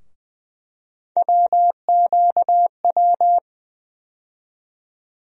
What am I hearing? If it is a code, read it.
Morse code, heard as a keyed tone, about 20 wpm, 700 Hz. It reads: WQW
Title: USA Call Sign Postfixes - 3 Letters 20wpm